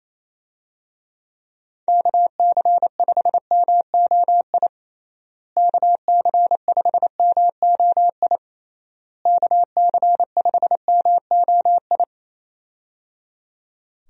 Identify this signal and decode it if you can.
Morse code, heard as a keyed tone, about 28 wpm, 700 Hz. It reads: KC5MOI KC5MOI KC5MOI
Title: USA Call Signs - 6 Characters - Repeated 3x - 28wpm